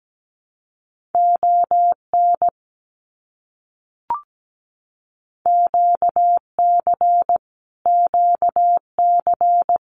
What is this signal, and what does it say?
Morse code, heard as a keyed tone, about 17 wpm, 700 Hz. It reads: ON QC QC
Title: Canadian Provinces and Territories - Repeated 3x - 17wpm